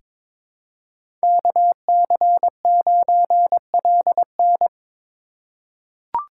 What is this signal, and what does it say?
Morse code, heard as a keyed tone, about 22 wpm, 700 Hz. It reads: KC9LN
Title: USA Call Signs - 8x Character Spacing - 22wpm